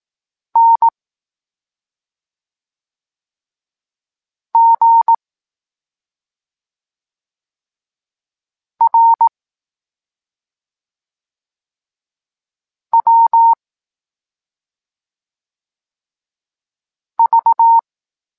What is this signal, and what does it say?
Morse code, heard as a keyed tone, about 18 wpm, 920 Hz. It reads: N G R W V
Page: Tekst 23.3 – Ciąg dalszy doskonalenia odbioru poznanych liter